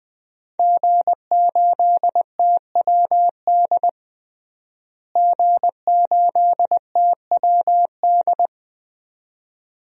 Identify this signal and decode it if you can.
Morse code, heard as a keyed tone, about 21 wpm, 700 Hz. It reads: G8TWD G8TWD
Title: DXCC-100 Call Signs - 5 Characters - Repeated 2x - 20wpm